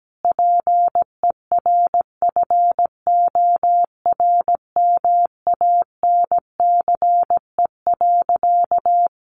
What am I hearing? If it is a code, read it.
Morse code, heard as a keyed tone, about 17 wpm, 700 Hz. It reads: PERFORMANCE.
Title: Sentences from Top 1100 Words 17wpm